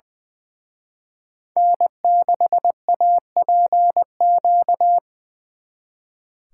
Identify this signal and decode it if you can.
Morse code, heard as a keyed tone, about 25 wpm, 700 Hz. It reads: N6APQ
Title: Speed Racing - USA 1x3 Call Signs 20wpm